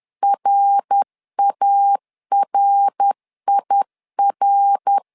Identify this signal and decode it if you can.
Morse code, heard as a keyed tone, about 10 wpm, 800 Hz. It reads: RARIR
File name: LEZIONI / 3 MWRO / 3 MWR 10wpm 1952